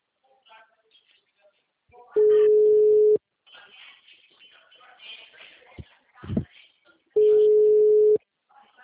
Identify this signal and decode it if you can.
Ringback tone (Europe, 425 Hz), with speech.
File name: out- -2003- - - .136